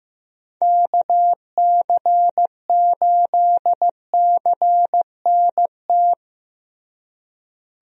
Morse code, heard as a keyed tone, about 16 wpm, 700 Hz. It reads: KC8CNT
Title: USA Call Signs - 6 Characters - Repeated 2x - 15wpm